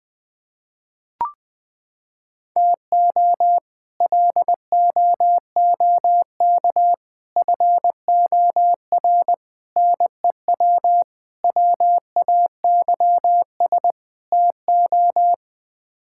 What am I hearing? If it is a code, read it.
Morse code, heard as a keyed tone, about 20 wpm, 700 Hz. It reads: TO LOOK FOR NEW WAYS TO
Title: Sets of 6 Words - 20wpm